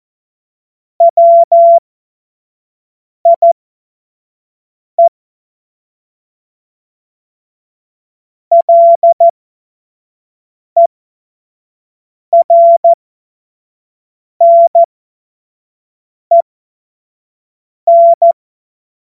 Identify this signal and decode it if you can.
Morse code, heard as a keyed tone, about 14 wpm, spaced as for 5 wpm, 680 Hz. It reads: WIE LERNEN
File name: Morsekurs / Witze / Witz 10 Juni 2025 25 F70